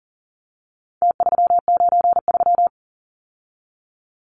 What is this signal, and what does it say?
Morse code, heard as a keyed tone, about 40 wpm, 700 Hz. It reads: T393
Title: Numbers - Contest Style - 4-Digit 40wpm